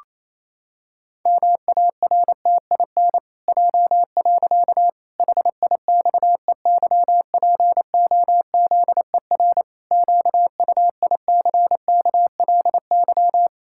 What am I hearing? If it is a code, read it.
Morse code, heard as a keyed tone, about 28 wpm, 700 Hz. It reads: MARTIN J. HIXEYPOZER QUICKLY
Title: Pangram Challenge v2 - 28wpm